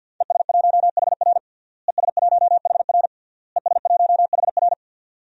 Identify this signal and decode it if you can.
Morse code, heard as a keyed tone, about 52 wpm, 700 Hz. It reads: ES1HR ES1HR ES1HR
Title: DXCC-100 Call Signs - 5 Characters - Repeated 3x - 50wpm